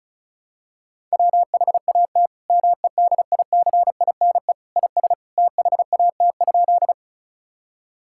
Morse code, heard as a keyed tone, about 36 wpm, 700 Hz. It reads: WHAT MEDICINE IS THAT?
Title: Sentences from Top 1000 Words - Review Part 6 - 35wpm